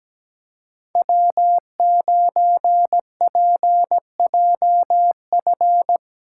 Morse code, heard as a keyed tone, about 17 wpm, 700 Hz. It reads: W9PJF
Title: USA Call Signs - 5 Characters - 3x Character Spacing - 17wpm